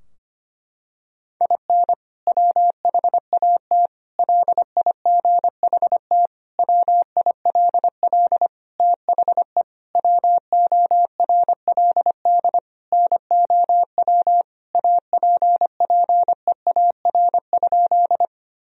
Morse code, heard as a keyed tone, about 25 wpm, 700 Hz. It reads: IN WHAT LIGHT WILL THE WORLD NOW APPEAR?
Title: Sentences from Top 1000 Words - Review Part 2 - 25wpm